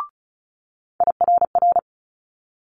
Morse code, heard as a keyed tone, about 36 wpm, 700 Hz. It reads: IRR
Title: Multi-letter Phonemes Quick Fire 35wpm